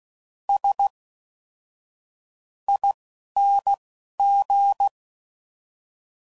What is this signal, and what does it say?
Morse code, heard as a keyed tone, about 16 wpm, spaced as for 11 wpm, 800 Hz. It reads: S ING
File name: Suffixes / Special INGs / Fast INGs